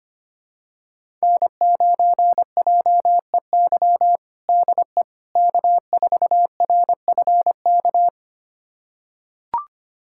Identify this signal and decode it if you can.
Morse code, heard as a keyed tone, about 25 wpm, 700 Hz. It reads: N9JEY DE K4RFK
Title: USA Callsign DE Callsign - 5 Characters - 25wpm